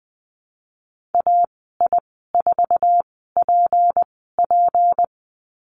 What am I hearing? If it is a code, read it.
Morse code, heard as a keyed tone, about 20 wpm, spaced as for 15 wpm, 700 Hz. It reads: AI4PP
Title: 179 Lesson 29g - Slash - Call Signs 20wpm